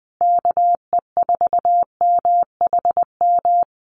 Morse code, heard as a keyed tone, about 20 wpm, 700 Hz. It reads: KE4MHM
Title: Common QSO Phrases Encore 20wpm